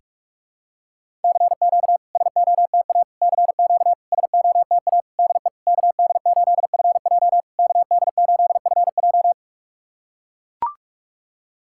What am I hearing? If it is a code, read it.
Morse code, heard as a keyed tone, about 44 wpm, 700 Hz. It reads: CQ SOTA CQ SOTA DE KD8FJ KD8FJ
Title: SOTA QSO Practice 45wpm